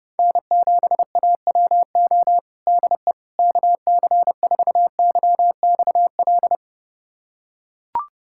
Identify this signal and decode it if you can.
Morse code, heard as a keyed tone, about 30 wpm, 700 Hz. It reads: N7AWO DE KC4YXL
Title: USA Callsign DE Callsign 30wpm